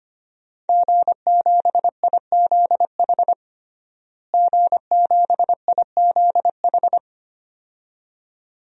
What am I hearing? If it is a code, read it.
Morse code, heard as a keyed tone, about 25 wpm, 700 Hz. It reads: G7IZH G7IZH
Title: DXCC-100 Call Signs - 5 Characters - Repeated 2x - 25wpm